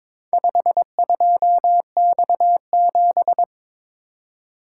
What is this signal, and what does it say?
Morse code, heard as a keyed tone, about 22 wpm, 700 Hz. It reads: 52X7
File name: practice / 22wpm / Four Letter-Number Combinations - 22wpm